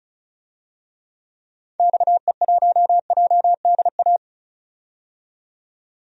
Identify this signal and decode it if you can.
Morse code, heard as a keyed tone, about 36 wpm, 700 Hz. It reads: XE1JDA
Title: DXCC-100 Call Signs - 6 Characters - 35wpm